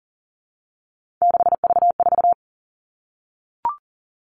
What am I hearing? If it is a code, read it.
Morse code, heard as a keyed tone, about 40 wpm, 700 Hz. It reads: 6V4
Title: All Cut Numbers - 3-Digit 40wpm